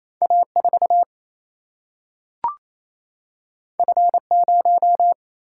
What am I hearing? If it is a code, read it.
Morse code, heard as a keyed tone, about 28 wpm, 700 Hz. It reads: A4 F0
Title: Single-Letter Single-Number 28wpm